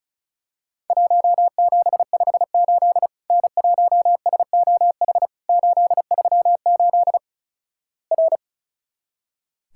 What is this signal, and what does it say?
Morse code, heard as a keyed tone, about 36 wpm, 700 Hz. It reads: 1758 N1SOH 838 R
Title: Worked All Europe DX Contest - QTC Exchange - 35wpm